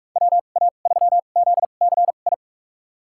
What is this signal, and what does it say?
Morse code, heard as a keyed tone, about 45 wpm, spaced as for 32 wpm, 700 Hz. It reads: WA3ZCI
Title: USA Call Signs - 6 Characters - 2x Character Spacing - 45wpm